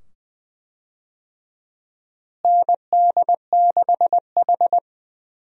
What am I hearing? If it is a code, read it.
Morse code, heard as a keyed tone, about 20 wpm, 700 Hz. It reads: ND6H
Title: USA Call Signs - 4 Characters - Repeated 2x - 20wpm